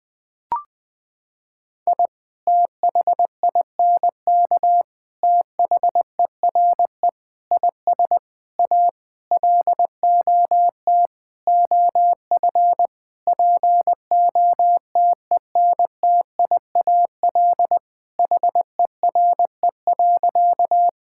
Morse code, heard as a keyed tone, about 20 wpm, 700 Hz. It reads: I THINK THERE IS A LOT OF POTENTIAL HERE.
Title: Sentences with 6-Word Phrases - 20wpm